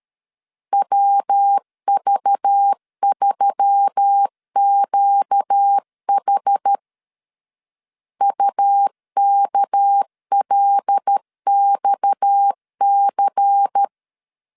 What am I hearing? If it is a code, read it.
Morse code, heard as a keyed tone, about 13 wpm, 800 Hz. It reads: WV3QH UKLXC
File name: CW / DET / 1 6 1500C S B 12wpm 3431